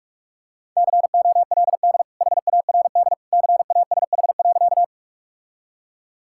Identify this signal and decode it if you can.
Morse code, heard as a keyed tone, about 46 wpm, 700 Hz. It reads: COLD HARD CASH.
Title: Sentences from Top 1000 Words - Review Part 5 - 45wpm